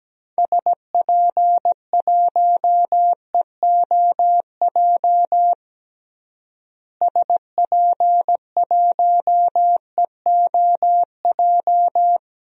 Morse code, heard as a keyed tone, about 17 wpm, 700 Hz. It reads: SP1EOJ SP1EOJ
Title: DXCC-100 Call Signs - 6 Characters - Repeated 3x - 17wpm